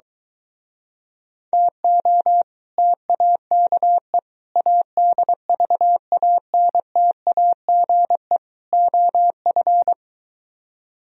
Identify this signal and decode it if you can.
Morse code, heard as a keyed tone, about 24 wpm, 700 Hz. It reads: TO TAKE ADVANTAGE OF